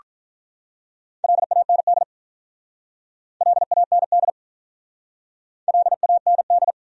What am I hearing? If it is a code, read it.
Morse code, heard as a keyed tone, about 47 wpm, 700 Hz. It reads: LAND LAND LAND